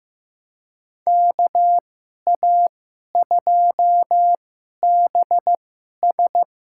Morse code, heard as a keyed tone, about 15 wpm, spaced as for 11 wpm, 700 Hz. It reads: KA2BS
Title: USA Call Signs - 2x Character Spacing - 15wpm